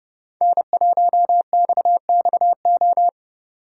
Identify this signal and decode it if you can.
Morse code, heard as a keyed tone, about 32 wpm, 700 Hz. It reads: N1XXO
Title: USA Call Signs - 5 Characters - 6x Character Spacing - 30wpm